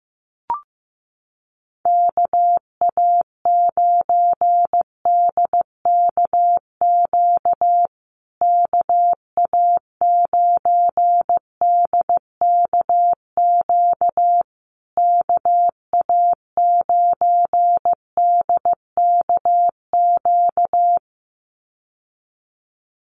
Morse code, heard as a keyed tone, about 15 wpm, 700 Hz. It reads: KA9DKQ KA9DKQ KA9DKQ
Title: USA Call Signs Repeated 3x 15wpm